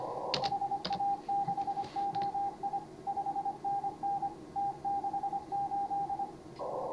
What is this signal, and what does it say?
Morse code, heard as a keyed tone, about 26 wpm, 810 Hz, with radio noise. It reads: HA6OI 5NN T68
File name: CW / HA5MA RHOB HANG CW 2017